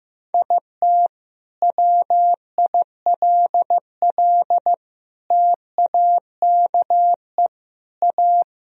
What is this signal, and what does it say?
Morse code, heard as a keyed tone, about 15 wpm, 700 Hz. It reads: IT WILL TAKE A
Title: Sentences from Top 1000 Words - Review Part 4 - 15wpm